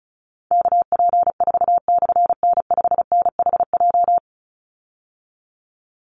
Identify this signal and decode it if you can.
Morse code, heard as a keyed tone, about 34 wpm, 700 Hz. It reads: KP4/N5NHJ
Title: Call Signs - 35wpm